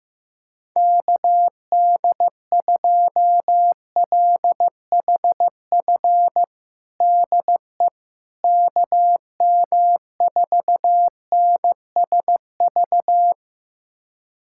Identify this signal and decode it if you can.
Morse code, heard as a keyed tone, about 15 wpm, 700 Hz. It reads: KD2LHF DE KM4NSV
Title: USA Callsign DE Callsign - 6 Characters - 15wpm